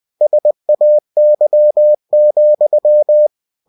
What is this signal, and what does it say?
Morse code, heard as a keyed tone, about 20 wpm, 600 Hz. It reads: SAY,
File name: cw-books / winnie-the-pooh / winnie-20wpm-page-0102